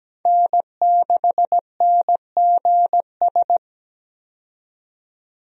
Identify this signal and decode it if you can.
Morse code, heard as a keyed tone, about 17 wpm, 700 Hz. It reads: N6NGS
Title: USA Call Signs - 5 Characters - 17wpm